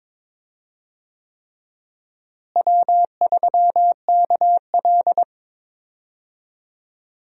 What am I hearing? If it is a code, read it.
Morse code, heard as a keyed tone, about 22 wpm, 700 Hz. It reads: W3KL
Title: USA Call Signs - 4 Characters - Repeated 3x - 22wpm